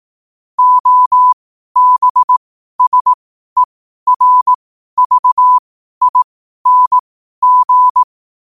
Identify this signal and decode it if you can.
Morse code, heard as a keyed tone, about 18 wpm, spaced as for 12 wpm, 1000 Hz. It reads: OBSERVING
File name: qotdfiles / 12wpm